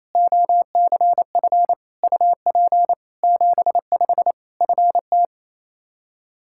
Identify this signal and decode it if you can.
Morse code, heard as a keyed tone, about 28 wpm, 700 Hz. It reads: OCF UP 75 FT
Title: K1USN SST QSO Practice 28wpm